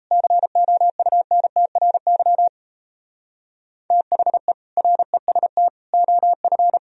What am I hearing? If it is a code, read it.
Morse code, heard as a keyed tone, about 37 wpm, 700 Hz. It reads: COUNTRY THE REST OF